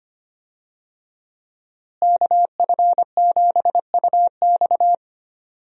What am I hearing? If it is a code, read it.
Morse code, heard as a keyed tone, about 25 wpm, 700 Hz. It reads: KF7UX
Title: USA Call Signs - 5 Characters - Repeated 3x - 25wpm